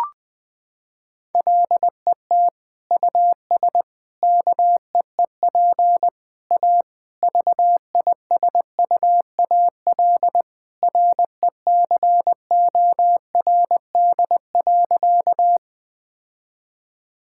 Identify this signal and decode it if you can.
Morse code, heard as a keyed tone, about 21 wpm, 700 Hz. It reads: LET US KEEP A VISUAL RECORD.
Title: Sentences from Top 1000 Words - Review Part 6 - 20wpm